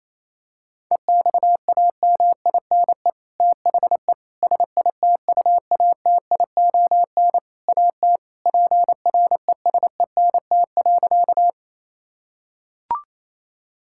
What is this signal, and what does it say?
Morse code, heard as a keyed tone, about 28 wpm, 700 Hz. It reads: EXAMINE THE SITUATION AT PRESENT.
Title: Sentences from Top 1600 Words 28wpm